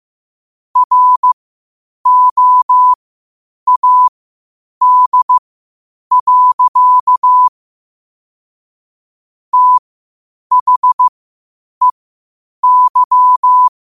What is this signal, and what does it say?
Morse code, heard as a keyed tone, about 15 wpm, spaced as for 8 wpm, 1000 Hz. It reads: ROAD. THEY
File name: qotdfiles / 8wpm